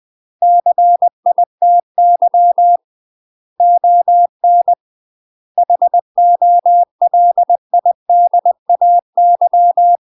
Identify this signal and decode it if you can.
Morse code, heard as a keyed tone, about 20 wpm, 700 Hz. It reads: CITY ON HOLIDAY
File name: UK MASTER QSOs TRAINING / 20WPM / UK MASTER QSOs 0042